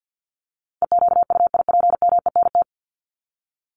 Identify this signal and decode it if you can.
Morse code, heard as a keyed tone, about 50 wpm, 700 Hz. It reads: EQUIPMENT
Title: Top 501-750 Words 50wpm